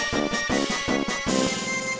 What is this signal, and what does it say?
There is no code.